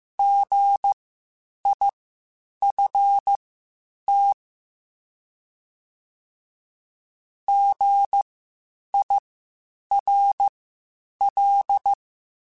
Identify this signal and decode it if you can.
Morse code, heard as a keyed tone, about 15 wpm, spaced as for 8 wpm, 800 Hz. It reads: GIFT GIRL
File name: Spaced Alphas / Spaced Alphas mp3s / Gs sp @13